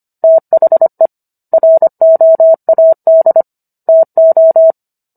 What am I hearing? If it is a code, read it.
Morse code, heard as a keyed tone, about 25 wpm, 650 Hz. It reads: THE ROAD TO